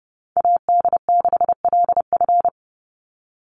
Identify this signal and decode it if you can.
Morse code, heard as a keyed tone, about 30 wpm, 700 Hz. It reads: AD6LF
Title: USA Call Signs 30wpm